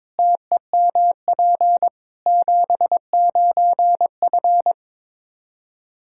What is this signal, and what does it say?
Morse code, heard as a keyed tone, about 21 wpm, 700 Hz. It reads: TEMP 79F
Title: Practice QSO - Exchange 2 - Info - 22wpm